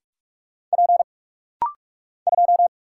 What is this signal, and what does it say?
Morse code, heard as a keyed tone, about 44 wpm, 700 Hz. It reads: P 2
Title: Single Letter-Number - Rapid-Fire 45wpm